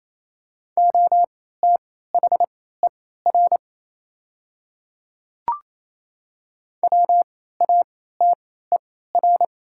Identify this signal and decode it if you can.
Morse code, heard as a keyed tone, about 28 wpm, spaced as for 16 wpm, 700 Hz. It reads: OTHER WATER
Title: 3x Character Spacing - Top 100 Words - 28wpm